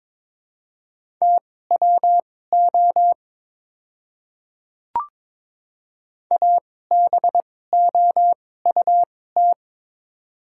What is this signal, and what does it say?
Morse code, heard as a keyed tone, about 22 wpm, spaced as for 16 wpm, 700 Hz. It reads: TWO ABOUT
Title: 2x Character Spacing - Top 100 Words - 22wpm